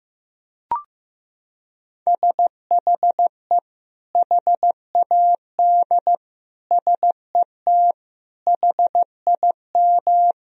Morse code, heard as a keyed tone, about 15 wpm, 700 Hz. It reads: SHE HAD SET HIM
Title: Sentences from Top 1000 Words - Review Part 1 - 15wpm